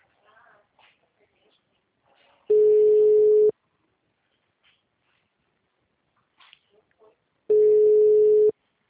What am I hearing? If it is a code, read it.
Ringback tone (Europe, 425 Hz).